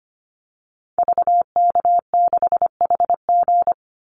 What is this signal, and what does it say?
Morse code, heard as a keyed tone, about 25 wpm, 700 Hz. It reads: VK6HG
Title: Call Signs Repeated 3x - 25wpm